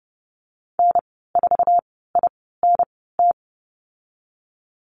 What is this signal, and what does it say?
Morse code, heard as a keyed tone, about 30 wpm, spaced as for 17 wpm, 700 Hz. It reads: N4INT
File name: Lesson 15g - Number 5 - Call Signs 30wpm